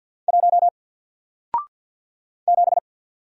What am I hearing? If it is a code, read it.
Morse code, heard as a keyed tone, about 48 wpm, 700 Hz. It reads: 1 7
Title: Single Character (Letter:Number:Punctuation:Prosign) 50wpm